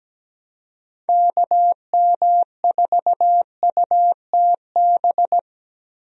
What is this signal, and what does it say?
Morse code, heard as a keyed tone, about 17 wpm, 700 Hz. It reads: KM4UTB
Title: USA Call Signs - 6 Characters - Repeated 2x - 17wpm